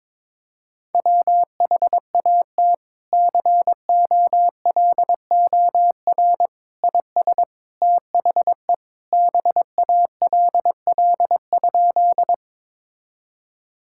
Morse code, heard as a keyed tone, about 22 wpm, 700 Hz. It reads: WHAT COLOR IS THE BALL?
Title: Sentences from Top 1000 Words - Review Part 6 - 22wpm